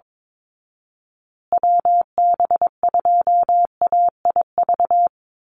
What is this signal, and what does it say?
Morse code, heard as a keyed tone, about 22 wpm, 700 Hz. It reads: WB2AIV
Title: Call Signs Repeated 2x - 22wpm